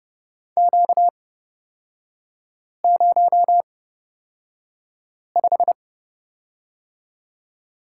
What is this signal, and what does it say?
Morse code, heard as a keyed tone, about 31 wpm, spaced as for 5 wpm, 690 Hz. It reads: Q05
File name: Lesson 38e - Letter q - Three Characters 30wpm